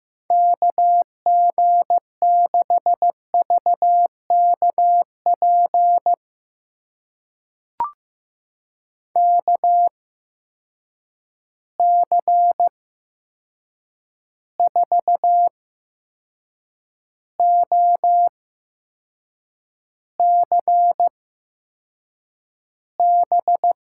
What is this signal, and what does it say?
Morse code, heard as a keyed tone, about 15 wpm, spaced as for 6 wpm, 700 Hz. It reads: KG6VKP KC4OCB
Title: USA Call Signs - 6 Characters - 8x Character Spacing - 15wpm